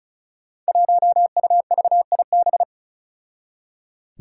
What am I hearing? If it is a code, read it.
Morse code, heard as a keyed tone, about 34 wpm, 700 Hz. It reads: 1UVIB
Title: Five Challenging Letter-Number Combinations 35wpm